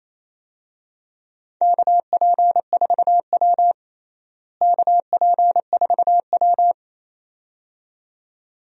Morse code, heard as a keyed tone, about 28 wpm, 700 Hz. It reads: KP4W KP4W
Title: DXCC-100 Call Signs - 4 Characters - Repeated 2x - 28wpm